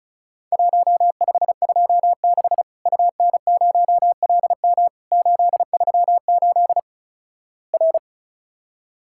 Morse code, heard as a keyed tone, about 36 wpm, 700 Hz. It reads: 1526 UN0LM 838 R
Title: Worked All Europe DX Contest - QTC Exchange - 35wpm